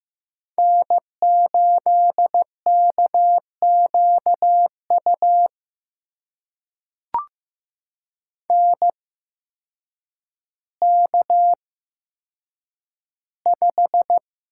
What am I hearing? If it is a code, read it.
Morse code, heard as a keyed tone, about 15 wpm, spaced as for 8 wpm, 700 Hz. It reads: N8KQU NK5
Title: USA Call Signs - 8x Character Spacing - 15wpm